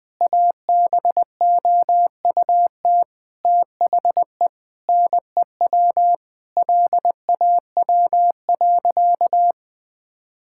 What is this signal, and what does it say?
Morse code, heard as a keyed tone, about 21 wpm, 700 Hz. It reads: ABOUT THE NEW LAW.
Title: Sentences with 8-Word Phrases - 20wpm